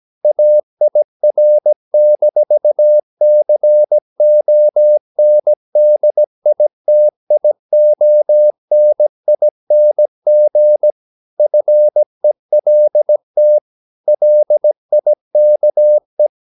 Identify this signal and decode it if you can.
Morse code, heard as a keyed tone, about 17 wpm, 600 Hz. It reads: AIR-CONDITIONING FELT LIKE
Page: Romanen Never af Ken Follet på engelsk.